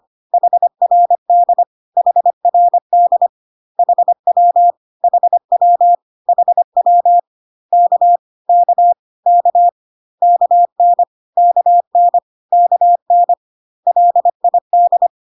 Morse code, heard as a keyed tone, about 25 wpm, 700 Hz. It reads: HRD HRD HW HW HW K K K KN KN KN LID
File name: cw / CW common abbreviations alphabetical repeated 3 times 25 25wpm ik1hge